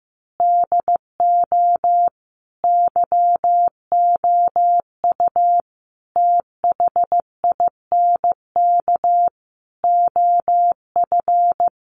Morse code, heard as a keyed tone, about 15 wpm, 700 Hz. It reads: DO YOU THINK OF
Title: Sentences from Top 1200 Words 15wpm